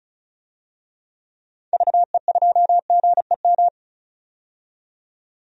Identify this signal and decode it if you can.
Morse code, heard as a keyed tone, about 36 wpm, 700 Hz. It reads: VE2GEM
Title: DXCC-100 Call Signs - 6 Characters - 35wpm